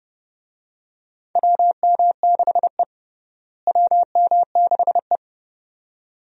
Morse code, heard as a keyed tone, about 30 wpm, 700 Hz. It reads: WM6E WM6E
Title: USA Call Signs - 4 Characters - Repeated 2x - 30wpm